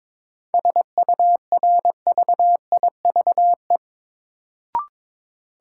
Morse code, heard as a keyed tone, about 22 wpm, 700 Hz. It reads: SURVIVE
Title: Sentences from Top 1300 Words - New Word Drill - 22wpm